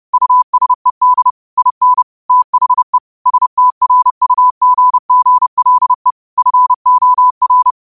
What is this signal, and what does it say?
Morse code, heard as a keyed tone, about 30 wpm, 1000 Hz. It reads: USED IN THE STRUGGLE FOR